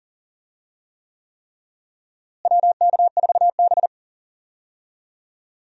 Morse code, heard as a keyed tone, about 40 wpm, 700 Hz. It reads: WK4B
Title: USA Call Signs - 4 Characters - Repeated 3x - 40wpm